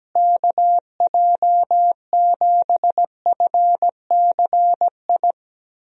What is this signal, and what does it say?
Morse code, heard as a keyed tone, about 17 wpm, 700 Hz. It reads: KJ7FCI
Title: USA Call Signs - 6 Characters - Repeated 3x - 17wpm